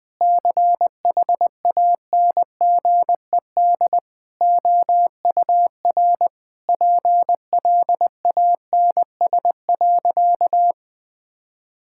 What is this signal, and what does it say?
Morse code, heard as a keyed tone, about 20 wpm, 700 Hz. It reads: CHANGED OUR PLANS.
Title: Sentences with 5-Word Phrases - Spoken Beginning - 20wpm